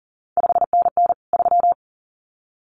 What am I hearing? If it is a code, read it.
Morse code, heard as a keyed tone, about 39 wpm, 700 Hz. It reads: 5NN 3
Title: CQ WW DX Contest Exchange Encore - 40wpm